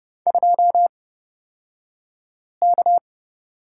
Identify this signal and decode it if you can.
Morse code, heard as a keyed tone, about 30 wpm, spaced as for 5 wpm, 700 Hz. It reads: 2K
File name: Lesson 37d - Letter x - Two Characters 30wpm